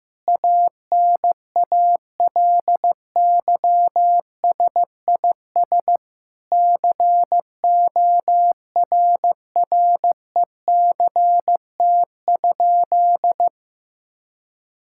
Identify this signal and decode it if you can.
Morse code, heard as a keyed tone, about 16 wpm, 700 Hz. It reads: ANALYSIS CORRECT?
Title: Sentences from Top 1000 Words - Review Part 5 - 15wpm